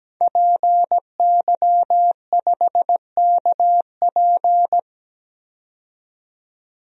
Morse code, heard as a keyed tone, about 17 wpm, 700 Hz. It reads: PY5KP
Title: DXCC-100 Call Signs - 5 Characters - 17wpm